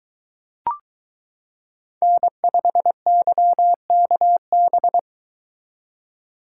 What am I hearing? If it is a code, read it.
Morse code, heard as a keyed tone, about 22 wpm, 700 Hz. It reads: N5YKB